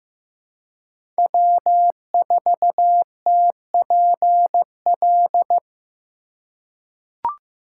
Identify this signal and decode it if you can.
Morse code, heard as a keyed tone, about 15 wpm, 700 Hz. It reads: W4TPL
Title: USA Call Signs - 5 Characters - 3x Character Spacing - 15wpm